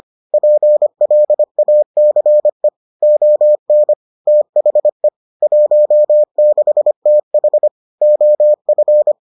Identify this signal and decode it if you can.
Morse code, heard as a keyed tone, about 25 wpm, 600 Hz. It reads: PLACE ON THE 16TH OF